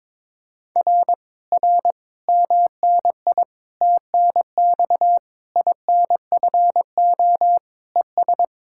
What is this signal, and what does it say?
Morse code, heard as a keyed tone, about 22 wpm, 700 Hz. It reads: R R MNI TNX INFO ES
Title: Speed-Racing Common QSO Phrases - 22wpm